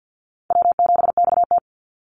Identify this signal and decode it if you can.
Morse code, heard as a keyed tone, about 50 wpm, 700 Hz. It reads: W7XT
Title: Easy Call Signs - 50wpm